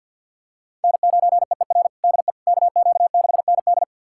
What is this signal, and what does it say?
Morse code, heard as a keyed tone, about 51 wpm, 700 Hz. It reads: N9EER DE KQ6ND
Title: USA Callsign DE Callsign - 5 Characters - 50wpm